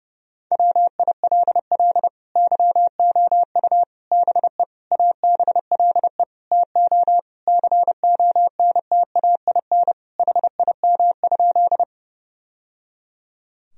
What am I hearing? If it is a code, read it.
Morse code, heard as a keyed tone, about 30 wpm, 700 Hz. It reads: WILL YOU BE ABLE TO CONTAIN HIM?
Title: Sentences from Top 1000 Words - Review Part 2 - 30wpm